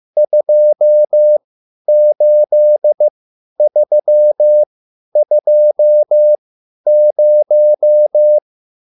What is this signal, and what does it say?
Morse code, heard as a keyed tone, about 15 wpm, spaced as for 10 wpm, 600 Hz. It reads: 28320
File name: QSL / CW lesson